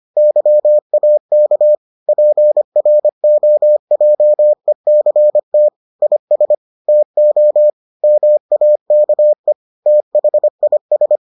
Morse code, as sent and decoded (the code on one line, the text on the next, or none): -.-- .- -.- / .--. .-. --- .--- . -.-. - / .. ... / - --- / -- .- -.- . / - .... .. ...
YAK PROJECT IS TO MAKE THIS